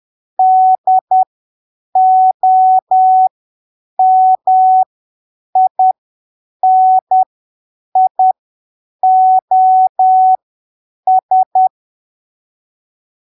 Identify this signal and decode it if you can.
Morse code, heard as a keyed tone, about 10 wpm, spaced as for 7 wpm, 750 Hz. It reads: DOMINIOS
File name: arquivos / texto / texto 25 10ppm